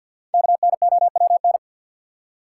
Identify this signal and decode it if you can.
Morse code, heard as a keyed tone, about 47 wpm, 700 Hz. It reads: KNOWN
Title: Five Letter Words - 50wpm